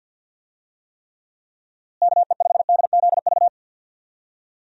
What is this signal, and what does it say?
Morse code, heard as a keyed tone, about 49 wpm, 700 Hz. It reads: KE5DZV
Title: USA Call Signs - 6 Characters - Repeated 2x - 50wpm